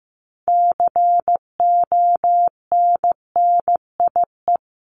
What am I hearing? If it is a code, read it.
Morse code, heard as a keyed tone, about 15 wpm, 700 Hz. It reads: CONNIE
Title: CWT Exchange 15wpm